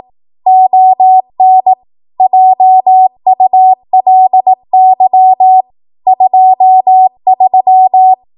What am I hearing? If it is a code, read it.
Morse code, heard as a keyed tone, about 18 wpm, 750 Hz. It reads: ON JULY 23
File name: Morse / 18WPM